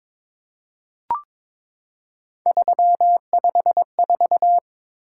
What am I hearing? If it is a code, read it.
Morse code, heard as a keyed tone, about 22 wpm, 700 Hz. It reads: 354
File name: practice / 22wpm / Three Letter-Number Combinations - 22wpm